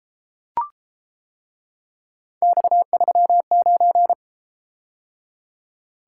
Morse code, heard as a keyed tone, about 31 wpm, 700 Hz. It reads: X39